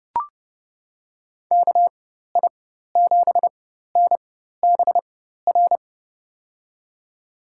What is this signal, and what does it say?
Morse code, heard as a keyed tone, about 30 wpm, spaced as for 14 wpm, 700 Hz. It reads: KI7NBR
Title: USA Call Signs - 6 Characters - 4x Character Spacing - 30wpm